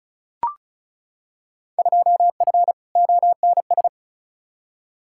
Morse code, heard as a keyed tone, about 34 wpm, 700 Hz. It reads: 2F ONS
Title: Field Day Exchanges - 35wpm